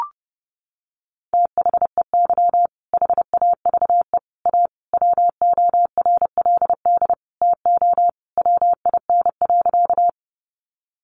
Morse code, heard as a keyed tone, about 30 wpm, 700 Hz. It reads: THEY HAVE A WORLD TO WIN.
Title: Sentences from Top 1500 Words 30wpm